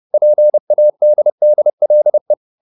Morse code, heard as a keyed tone, about 30 wpm, 600 Hz. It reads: PADDLE